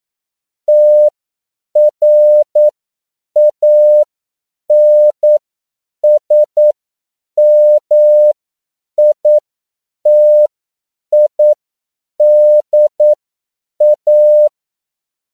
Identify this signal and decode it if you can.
Morse code, heard as a keyed tone, about 9 wpm, spaced as for 7 wpm, 600 Hz. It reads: TRANSMITIDA
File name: CW / 10PPM